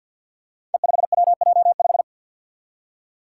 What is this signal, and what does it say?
Morse code, heard as a keyed tone, about 52 wpm, 700 Hz. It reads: E5WJ5
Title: Five Challenging Letter-Number Combinations 50wpm